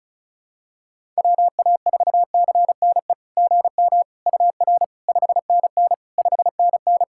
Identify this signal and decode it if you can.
Morse code, heard as a keyed tone, about 36 wpm, 700 Hz. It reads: WA4CNE GM UR 5NN 5NN
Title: SOTA QSO Practice 35wpm